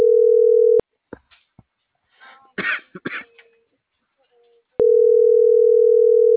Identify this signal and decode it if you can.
Ringback tone (North America), 440 + 480 Hz.